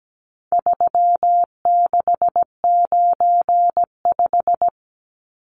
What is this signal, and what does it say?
Morse code, heard as a keyed tone, about 17 wpm, 700 Hz. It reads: 3695
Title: Numbers - 4-Digit 17wpm